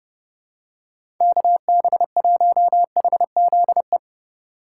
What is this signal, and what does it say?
Morse code, heard as a keyed tone, about 32 wpm, 700 Hz. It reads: KB1HZE
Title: USA Call Signs - 6 Characters - 3x Character Spacing - 30wpm